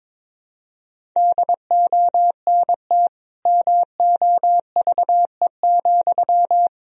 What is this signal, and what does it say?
Morse code, heard as a keyed tone, about 22 wpm, 700 Hz. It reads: DONT MOVE,
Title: Sentences from Top 1000 Words - Review Part 1 - 22wpm